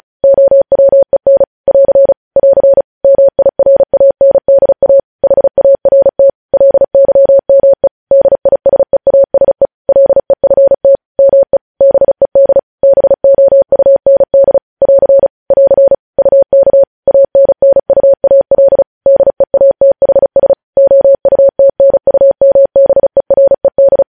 Morse code, heard as a keyed tone, about 36 wpm, 550 Hz. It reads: OWEN <AR> <AR> MIRANDA HART LYME DISEASE LEFT ME BED BOUND <AR> <AR> UK ANNUAL DEATHS OUTNUMBERED